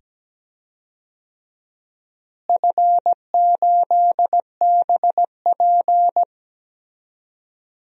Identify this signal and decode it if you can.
Morse code, heard as a keyed tone, about 17 wpm, 700 Hz. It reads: F8BP
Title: DXCC-100 Call Signs - 4 Characters - Repeated 2x - 17wpm